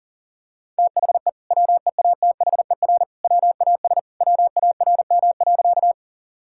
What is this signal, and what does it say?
Morse code, heard as a keyed tone, about 40 wpm, 700 Hz. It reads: THE WEATHER WAS WARM.
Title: Sentences from Top 1000 Words - Review Part 5 - 40wpm